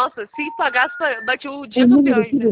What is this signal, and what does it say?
Special information tone, with speech.